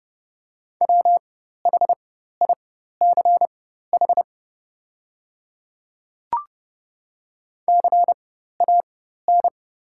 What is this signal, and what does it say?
Morse code, heard as a keyed tone, about 30 wpm, spaced as for 14 wpm, 700 Hz. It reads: WHICH CAN
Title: 4x Character Spacing - Top 200 Words - 30wpm